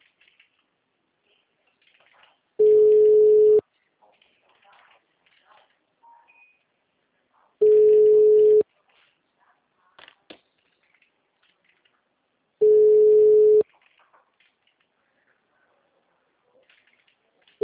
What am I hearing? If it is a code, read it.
Ringback tone (Europe, 425 Hz), with speech.
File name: out- -2002- - - .24674